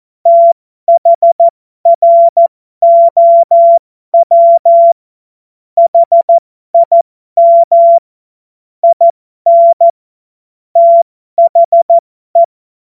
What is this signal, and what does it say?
Morse code, heard as a keyed tone, about 14 wpm, spaced as for 12 wpm, 680 Hz. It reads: THROW HIM IN THE
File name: Morsekurs / Witze